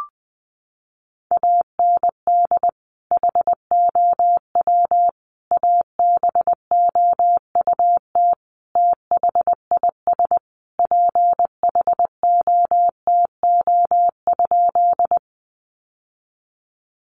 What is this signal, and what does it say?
Morse code, heard as a keyed tone, about 20 wpm, 700 Hz. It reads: AND HOW ABOUT THIS PHOTO?
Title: Sentences from Top 1200 Words 20wpm